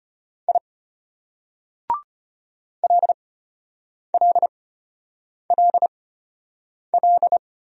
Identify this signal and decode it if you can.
Morse code, heard as a keyed tone, about 31 wpm, spaced as for 8 wpm, 700 Hz. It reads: I LLLL
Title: Speed-Racing Single Letters 25wpm